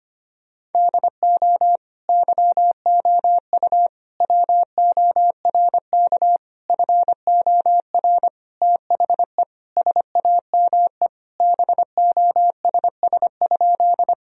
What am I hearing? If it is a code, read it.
Morse code, heard as a keyed tone, about 25 wpm, 700 Hz. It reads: DO YOU WORK FOR THE SAME BOSS?
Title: Sentences from Top 1000 Words - Review Part 4 - 25wpm